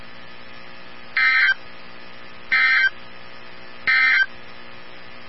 Emergency Alert System (SAME) tones, heard as harsh buzzing bursts. It is heard from 1.0 to 4.3 s.
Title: NOAA Weather Radio 162.550